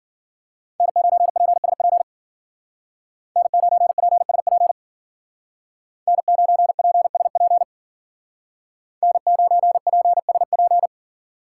Morse code, heard as a keyed tone, about 52 wpm, 700 Hz. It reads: N9PSP N9PSP N9PSP N9PSP